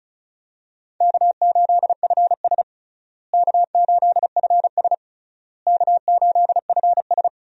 Morse code, heard as a keyed tone, about 36 wpm, 700 Hz. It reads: K8FS K8FS K8FS
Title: USA Call Signs - 4 Characters - Repeated 3x - 35wpm